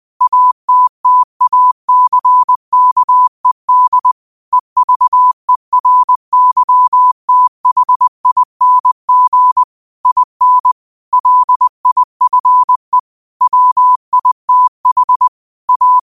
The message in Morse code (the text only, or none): ATTACKED EVERYTHING IN LIFE WITH A